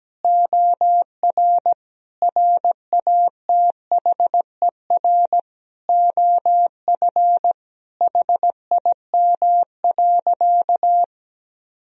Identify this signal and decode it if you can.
Morse code, heard as a keyed tone, about 17 wpm, 700 Hz. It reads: OR RATHER OF HIM.
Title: Sentences from Top 1000 Words - Review Part 4 - 17wpm